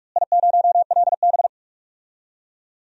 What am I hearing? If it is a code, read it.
Morse code, heard as a keyed tone, about 46 wpm, 700 Hz. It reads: I0LB
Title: DXCC-100 Call Signs - 4 Characters - Repeated 2x - 45wpm